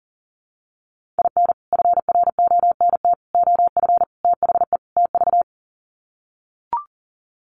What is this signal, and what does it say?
Morse code, heard as a keyed tone, about 40 wpm, 700 Hz. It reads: IN FRONT OF THE TV
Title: Sets of 5 Words Encore2 40wpm